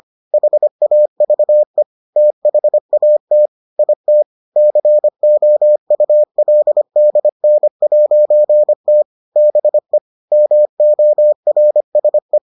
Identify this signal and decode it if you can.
Morse code, heard as a keyed tone, about 25 wpm, 600 Hz. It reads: HAVE THAT IT COULDN'T BE MORSE